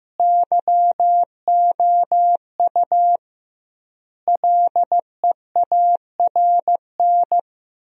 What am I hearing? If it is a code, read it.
Morse code, heard as a keyed tone, about 15 wpm, 700 Hz. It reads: YOU LEARN
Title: 2x Word-Spacing - Sentences from Top 300 Words - 15wpm